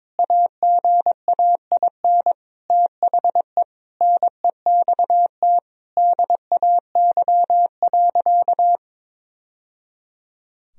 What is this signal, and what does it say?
Morse code, heard as a keyed tone, about 22 wpm, 700 Hz. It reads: AGAIN THE NEXT DAY.
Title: Sentences from Top 1000 Words - Review Part 1 - 22wpm